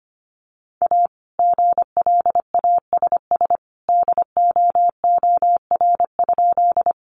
Morse code, heard as a keyed tone, about 25 wpm, 700 Hz. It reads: A GLASS DOOR?
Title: Sentences from Top 1300 Words 25wpm